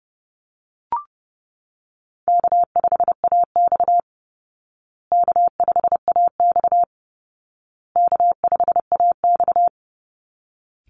Morse code, heard as a keyed tone, about 30 wpm, 700 Hz. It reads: K5AX K5AX K5AX
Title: Easy Call Signs Repeated 3x - 30wpm